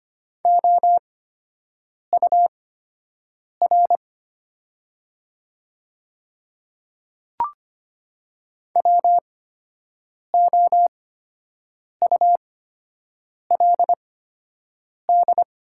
Morse code, heard as a keyed tone, about 25 wpm, spaced as for 7 wpm, 700 Hz. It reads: OUR WOULD
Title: 8x Character Spacing - Top 200 Words - 25wpm